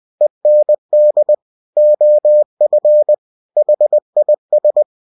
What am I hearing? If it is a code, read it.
Morse code, heard as a keyed tone, about 20 wpm, 600 Hz. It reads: END OF HIS
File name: cw-books / winnie-the-pooh / winnie-20wpm-page-0051